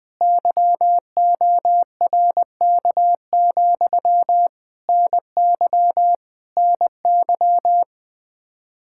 Morse code, heard as a keyed tone, about 20 wpm, 700 Hz. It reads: YORK, NY NY
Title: Practice QSO - Exchange 1 - Beginning - 20wpm